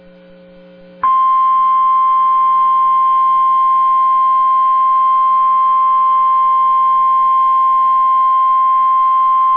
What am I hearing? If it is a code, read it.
Emergency Alert System (SAME) tones (the 1050 Hz attention tone). It is heard from 1.0 to 8.5 s.